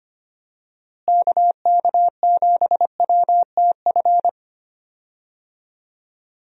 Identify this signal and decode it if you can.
Morse code, heard as a keyed tone, about 25 wpm, 700 Hz. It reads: KK7WTF
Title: USA Call Signs - 6 Characters - Repeated 2x - 25wpm